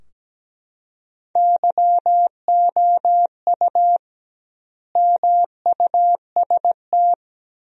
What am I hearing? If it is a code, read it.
Morse code, heard as a keyed tone, about 17 wpm, 700 Hz. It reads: YOU MUST
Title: 2x Word-Spacing - Sentences from Top 300 Words - 17wpm